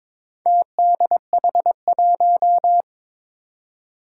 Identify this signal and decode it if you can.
Morse code, heard as a keyed tone, about 22 wpm, 700 Hz. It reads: TDH1
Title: Four Challenging Letter-Number Combinations 22wpm